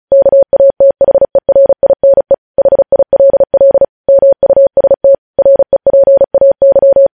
Morse code, heard as a keyed tone, about 36 wpm, 550 Hz. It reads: KATHERINE HILL MUST REPAY
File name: podcast / News headlines-1x-35x35